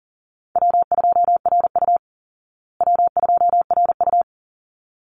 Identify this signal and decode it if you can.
Morse code, heard as a keyed tone, about 40 wpm, 700 Hz. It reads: W2RU W2RU
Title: Easy Call Signs Repeated 2x - 40wpm